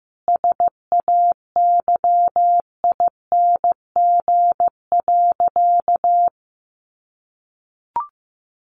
Morse code, heard as a keyed tone, about 15 wpm, 700 Hz. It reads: SAYING.
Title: A Sound Of Thunder 15wpm